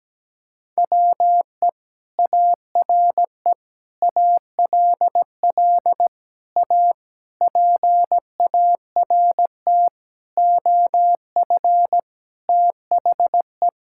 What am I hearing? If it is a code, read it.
Morse code, heard as a keyed tone, about 17 wpm, 700 Hz. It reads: WE ARE ALL A PART OF THE
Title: Sentences from Top 1000 Words - Review Part 6 - 17wpm